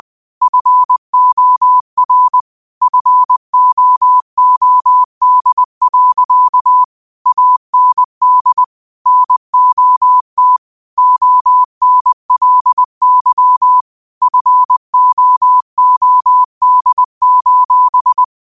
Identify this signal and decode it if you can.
Morse code, heard as a keyed tone, about 20 wpm, 1000 Hz. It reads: FOR FOOD. AND NOT ONLY FOOD: